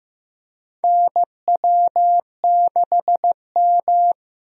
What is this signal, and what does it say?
Morse code, heard as a keyed tone, about 15 wpm, 700 Hz. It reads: NW6M
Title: USA Call Signs - 4 Characters - Repeated 3x - 15wpm